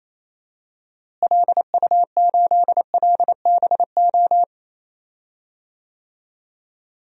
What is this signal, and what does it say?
Morse code, heard as a keyed tone, about 28 wpm, 700 Hz. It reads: LU8LBO
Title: DXCC-100 Call Signs - 6 Characters - Repeated 2x - 28wpm